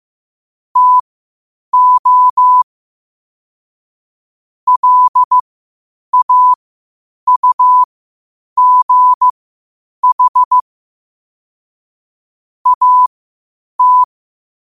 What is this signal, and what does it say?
Morse code, heard as a keyed tone, about 15 wpm, spaced as for 8 wpm, 1000 Hz. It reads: TO LAUGH AT
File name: qotdfiles / 8wpm